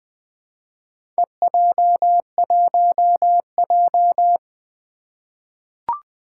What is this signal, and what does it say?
Morse code, heard as a keyed tone, about 20 wpm, 700 Hz. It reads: EJ1J
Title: Four Challenging Letter-Number Combinations 20wpm